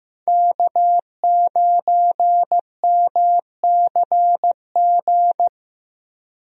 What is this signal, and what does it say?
Morse code, heard as a keyed tone, about 15 wpm, 700 Hz. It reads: K9MCG
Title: USA Call Signs - 5 Characters - Repeated 2x - 15wpm